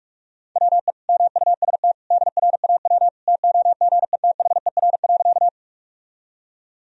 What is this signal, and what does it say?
Morse code, heard as a keyed tone, about 42 wpm, 700 Hz. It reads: WE MUST DRAW TOGETHER.
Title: Sentences from Top 1000 Words - Review Part 1 - 45wpm